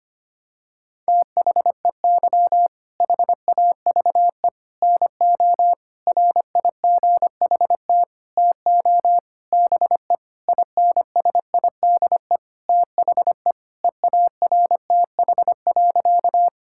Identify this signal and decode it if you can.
Morse code, heard as a keyed tone, about 25 wpm, 700 Hz. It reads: THEY HAVE NO RIGHT TO BE INSIDE THE EARTH.
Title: Sentences from Top 1000 Words - Review Part 6 - 25wpm